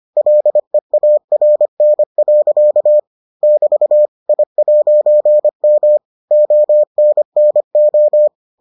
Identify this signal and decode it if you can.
Morse code, heard as a keyed tone, about 25 wpm, 600 Hz. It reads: LEARN. <BT> I'M ONNO